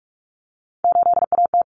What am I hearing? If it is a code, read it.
Morse code, heard as a keyed tone, about 44 wpm, 700 Hz. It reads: 8AT
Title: Common Cut Numbers - 3-Digit 45wpm